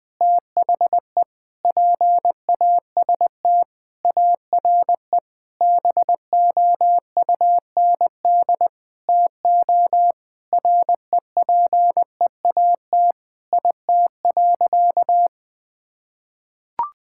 Morse code, heard as a keyed tone, about 20 wpm, 700 Hz. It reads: THE PAST ARE BOUND TO REPEAT IT.
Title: Sentences with 8-Word Phrases - 20wpm